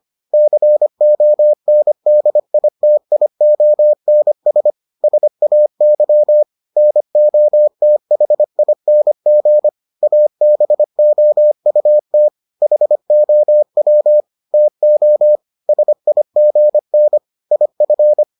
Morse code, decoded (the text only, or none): CONDITIONS SAY NOTHING ABOUT HOW TO SIGN IF